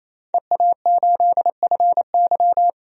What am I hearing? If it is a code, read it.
Morse code, heard as a keyed tone, about 28 wpm, 700 Hz. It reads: EA8FY
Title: DXCC-100 Call Signs - 5 Characters - 28wpm